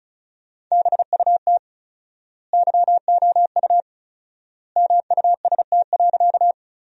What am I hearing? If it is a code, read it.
Morse code, heard as a keyed tone, about 34 wpm, 700 Hz. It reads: BUT YOU MUST.
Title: 4x Word-Spacing - Sentences from Top 200 Words - 35wpm